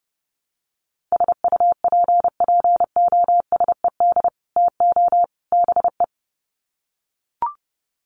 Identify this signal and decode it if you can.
Morse code, heard as a keyed tone, about 30 wpm, 700 Hz. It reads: SUPPOSED TO BE
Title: Sets of 3 Words 30wpm